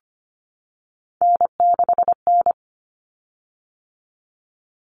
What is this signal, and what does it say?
Morse code, heard as a keyed tone, about 25 wpm, 700 Hz. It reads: N6N
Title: Common Cut Numbers - 3-Digit 25wpm